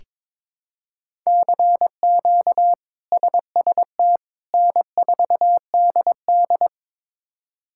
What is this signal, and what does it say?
Morse code, heard as a keyed tone, about 22 wpm, 700 Hz. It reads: CQ SST N4DD
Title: K1USN SST QSO Practice - 22wpm